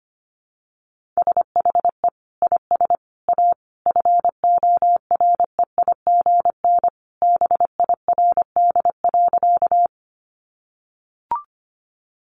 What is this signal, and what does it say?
Morse code, heard as a keyed tone, about 25 wpm, 700 Hz. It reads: SHE IS A FOREIGN BIRD.
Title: Sentences from Top 1400 Words 25wpm